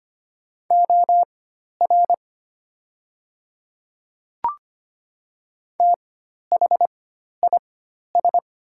Morse code, heard as a keyed tone, about 25 wpm, spaced as for 12 wpm, 700 Hz. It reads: OR THIS
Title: 4x Character Spacing - Top 100 Words - 25wpm